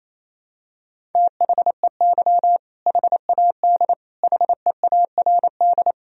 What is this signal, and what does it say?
Morse code, heard as a keyed tone, about 28 wpm, 700 Hz. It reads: THEY HAD HEARD
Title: Sentences from Top 400 Words and Abbreviations 28wpm